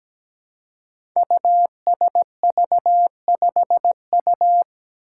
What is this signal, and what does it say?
Morse code, heard as a keyed tone, about 17 wpm, 700 Hz. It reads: USV5U
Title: Five Challenging Letter-Number Combinations 17wpm